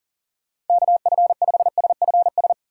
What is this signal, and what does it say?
Morse code, heard as a keyed tone, about 51 wpm, 700 Hz. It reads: KF5SFS